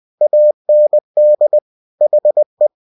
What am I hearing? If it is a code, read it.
Morse code, heard as a keyed tone, about 20 wpm, 600 Hz. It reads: AND HE
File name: cw-books / winnie-the-pooh / winnie-20wpm-page-0092